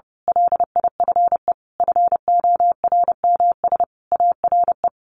Morse code, heard as a keyed tone, about 30 wpm, 700 Hz. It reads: LIFE FORMS ARE
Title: A Sound Of Thunder 30wpm